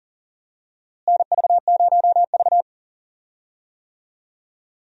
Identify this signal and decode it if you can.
Morse code, heard as a keyed tone, about 40 wpm, 700 Hz. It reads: NV0V
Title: USA Call Signs - 4 Characters - Repeated 2x - 40wpm